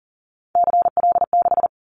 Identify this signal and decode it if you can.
Morse code, heard as a keyed tone, about 40 wpm, 700 Hz. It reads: CL6
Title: Callsign Prefixes for Top 100 DXCC - 40wpm